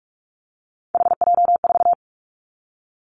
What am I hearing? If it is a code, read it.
Morse code, heard as a keyed tone, about 44 wpm, 700 Hz. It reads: HJ4
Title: Callsign Prefixes for Top 100 DXCC - 45wpm